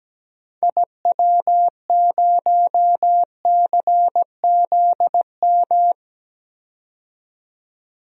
Morse code, heard as a keyed tone, about 17 wpm, 700 Hz. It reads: IW0CZM
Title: DXCC-100 Call Signs - 6 Characters - 17wpm